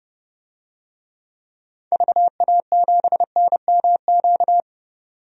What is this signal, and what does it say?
Morse code, heard as a keyed tone, about 31 wpm, 700 Hz. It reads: VA7NMQ
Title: DXCC-100 Call Signs - 6 Characters - 30wpm